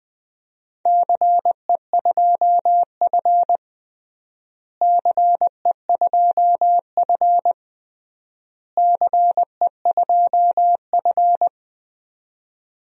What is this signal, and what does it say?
Morse code, heard as a keyed tone, about 20 wpm, 700 Hz. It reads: CE2F CE2F CE2F
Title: DXCC-100 Call Signs - 4 Characters - Repeated 3x - 20wpm